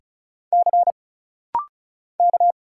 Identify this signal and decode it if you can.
Morse code, heard as a keyed tone, about 36 wpm, 700 Hz. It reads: C K
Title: Single Letters - Rapid-Fire 35wpm